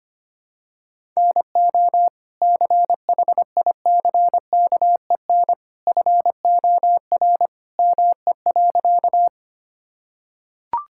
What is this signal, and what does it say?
Morse code, heard as a keyed tone, about 25 wpm, 700 Hz. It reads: NO CHICKEN FOR ME.
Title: Sentences from Top 1000 Words - Review Part 5 - 25wpm